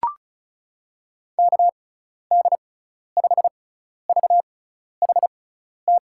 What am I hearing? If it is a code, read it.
Morse code, heard as a keyed tone, about 36 wpm, spaced as for 12 wpm, 700 Hz. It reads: KD5VHT
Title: USA Call Signs - 6 Characters - 6x Character Spacing - 35wpm